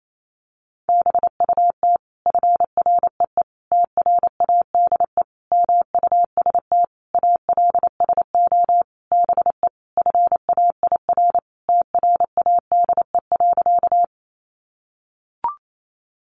Morse code, heard as a keyed tone, about 28 wpm, 700 Hz. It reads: BUT FREE TRADE MUST ALSO BE FAIR TRADE.
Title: Sentences from Top 1300 Words 28wpm